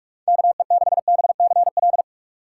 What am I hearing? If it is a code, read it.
Morse code, heard as a keyed tone, about 41 wpm, 690 Hz. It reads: KE6BCL
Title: USA Call Signs - 3x Character Spacing - 45wpm